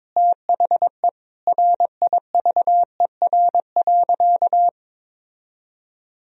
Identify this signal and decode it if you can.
Morse code, heard as a keyed tone, about 22 wpm, 700 Hz. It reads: THE RIVER.
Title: Sentences from Top 1000 Words - Review Part 4 - 22wpm